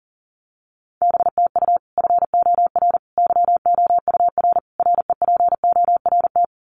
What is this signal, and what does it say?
Morse code, heard as a keyed tone, about 40 wpm, 700 Hz. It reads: BTU FOR YOUR REPORT
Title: Common QSO Phrases 40wpm